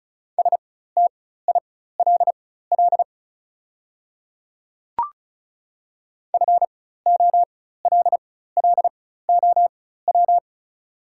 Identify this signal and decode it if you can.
Morse code, heard as a keyed tone, about 36 wpm, spaced as for 16 wpm, 700 Hz. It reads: STILL FOLLOW
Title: 4x Character Spacing - Top 300 Words - 35wpm